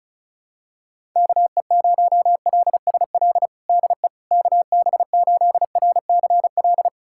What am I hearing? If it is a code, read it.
Morse code, heard as a keyed tone, about 36 wpm, 700 Hz. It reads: KE0LSL DE KB8RCL
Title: USA Callsign DE Callsign 35wpm